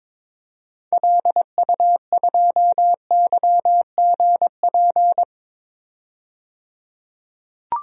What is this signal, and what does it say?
Morse code, heard as a keyed tone, about 22 wpm, 700 Hz. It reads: LU2YGP
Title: DXCC-100 Call Signs - 6 Characters - 22wpm